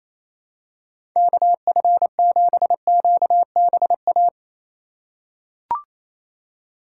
Morse code, heard as a keyed tone, about 28 wpm, 700 Hz. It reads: KF7QBA
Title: USA Call Signs - 2x Character Spacing - 28wpm